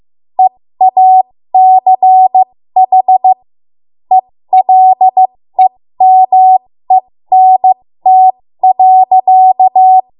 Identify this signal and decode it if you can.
Morse code, heard as a keyed tone, about 15 wpm, spaced as for 13 wpm, 750 Hz. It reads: EACH ELEMENT.
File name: Morse / 13WPM